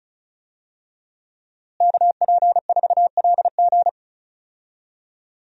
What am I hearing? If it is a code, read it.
Morse code, heard as a keyed tone, about 34 wpm, 700 Hz. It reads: KP4LG
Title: DXCC-100 Call Signs - 5 Characters - Repeated 2x - 35wpm